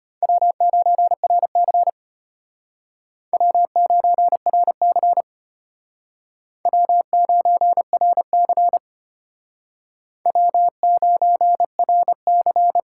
Morse code, heard as a keyed tone, about 32 wpm, 700 Hz. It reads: W9RC W9RC W9RC W9RC